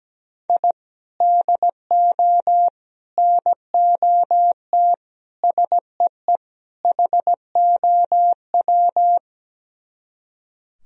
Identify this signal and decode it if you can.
Morse code, heard as a keyed tone, about 17 wpm, 700 Hz. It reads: I DO NOT SEE HOW
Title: Speed-Racing - Sets of 5 Words 17wpm